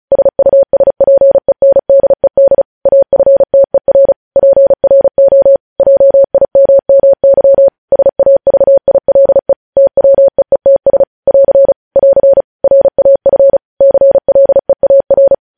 Morse code, heard as a keyed tone, about 36 wpm, 550 Hz. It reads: SUSPENDED AFTER PRO JIMMY SAVILE TWEETS <AR> <AR> RAF CLEAR
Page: news headlines in morse code at 35 wpm